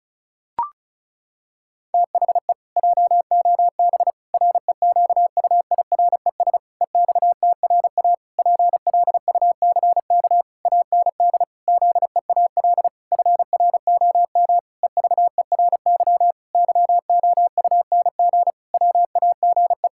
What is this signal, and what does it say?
Morse code, heard as a keyed tone, about 36 wpm, 700 Hz. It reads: THE JOB REQUIRES EXTRA PLUCK AND ZEAL FROM EVERY YOUNG WAGE
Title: Pangram Challenge v2 - 35wpm